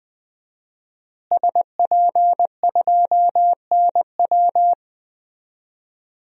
Morse code, heard as a keyed tone, about 21 wpm, 700 Hz. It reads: SP2NW
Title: DXCC-100 Call Signs - 5 Characters - Repeated 2x - 20wpm